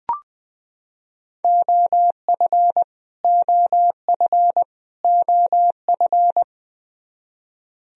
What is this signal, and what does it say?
Morse code, heard as a keyed tone, about 20 wpm, 700 Hz. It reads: OF OF OF
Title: Top 100 Words Repeated 20wpm